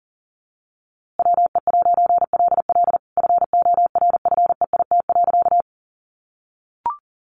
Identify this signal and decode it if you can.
Morse code, heard as a keyed tone, about 40 wpm, 700 Hz. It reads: WE'LL FORFEIT.
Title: A Sound Of Thunder 40wpm